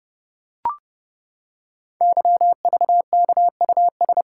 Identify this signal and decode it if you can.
Morse code, heard as a keyed tone, about 30 wpm, 700 Hz. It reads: YVKUS
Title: Code Groups - 5 Letters - 30wpm